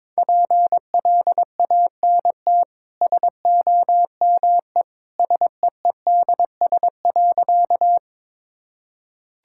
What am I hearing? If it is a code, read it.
Morse code, heard as a keyed tone, about 22 wpm, 690 Hz. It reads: PLANT SOME SEEDS.
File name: practice / 22wpm / Easy Sentences with Hints - 22wpm